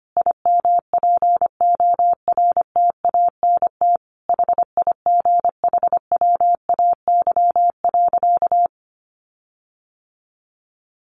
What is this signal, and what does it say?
Morse code, heard as a keyed tone, about 25 wpm, 700 Hz. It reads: IMPORTANT HIGHWAY.
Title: Sentences from Top 1400 Words 25wpm